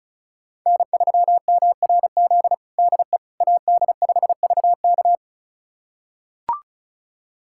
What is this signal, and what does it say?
Morse code, heard as a keyed tone, about 36 wpm, 700 Hz. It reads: N3MRZ DE AD5VK
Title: USA Callsign DE Callsign - 5 Characters - 35wpm